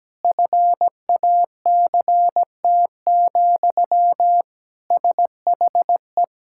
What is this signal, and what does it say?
Morse code, heard as a keyed tone, about 17 wpm, 700 Hz. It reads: FACT, SHE
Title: Sentences from Top 1000 Words - Review Part 6 - 17wpm